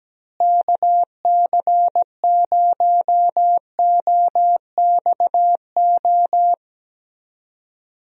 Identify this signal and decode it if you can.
Morse code, heard as a keyed tone, about 17 wpm, 700 Hz. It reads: KC0OXO
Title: USA Call Signs - 6 Characters - Repeated 2x - 17wpm